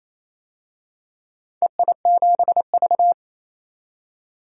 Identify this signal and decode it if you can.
Morse code, heard as a keyed tone, about 28 wpm, 700 Hz. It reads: EI7V
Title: DXCC-100 Call Signs - 4 Characters - 28wpm